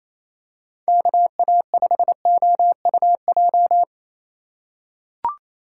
Morse code, heard as a keyed tone, about 28 wpm, 700 Hz. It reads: KA5OUJ
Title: USA Call Signs - 6 Characters - 2x Character Spacing - 28wpm